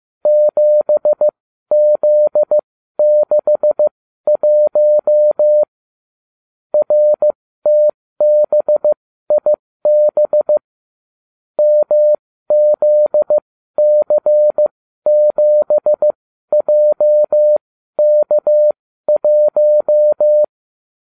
Morse code, heard as a keyed tone, about 15 wpm, spaced as for 12 wpm, 600 Hz. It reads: 7Z61 RTBIB MZC7JK1